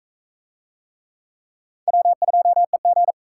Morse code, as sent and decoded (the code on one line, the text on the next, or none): .-- ..--- . --.
W2EG